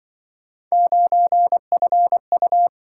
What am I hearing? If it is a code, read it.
Morse code, heard as a keyed tone, about 25 wpm, 700 Hz. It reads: 9FU